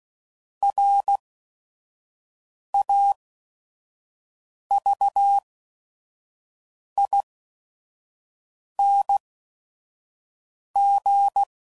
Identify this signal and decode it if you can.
Morse code, heard as a keyed tone, about 16 wpm, spaced as for 5 wpm, 800 Hz. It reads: RAVING
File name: Prefixes / By First Letter / Rs